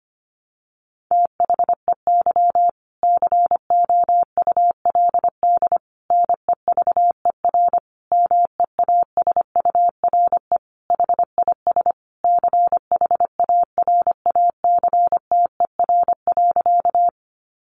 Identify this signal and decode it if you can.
Morse code, heard as a keyed tone, about 25 wpm, 700 Hz. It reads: THEY COULD NEVER MEASURE HIS CHARACTER.
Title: Sentences from Top 1200 Words 25wpm